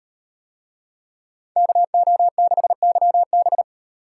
Morse code, heard as a keyed tone, about 38 wpm, 700 Hz. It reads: KO6YB